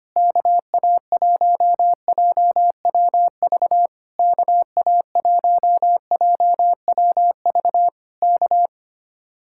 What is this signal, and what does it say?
Morse code, heard as a keyed tone, about 24 wpm, 700 Hz. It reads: KA1JWV KA1JWV K
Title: Speed-Racing Common QSO Phrases - 25wpm